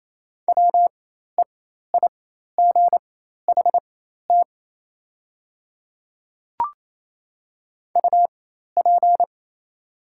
Morse code, heard as a keyed tone, about 28 wpm, spaced as for 13 wpm, 700 Hz. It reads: WEIGHT UP
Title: 4x Character Spacing - Top 500 Words - 28wpm